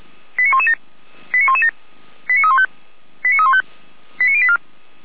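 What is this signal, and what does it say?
Five-tone selective call (ZVEI), heard as quick runs of five tones, tones of about 70 ms. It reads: 89117 89117 89315 89315 89084
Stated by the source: ZVEI five-tone selective call 89035, received by SDR.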